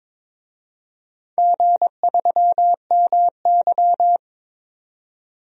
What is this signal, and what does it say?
Morse code, heard as a keyed tone, about 22 wpm, 700 Hz. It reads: G3MY
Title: DXCC-100 Call Signs - 4 Characters - 22wpm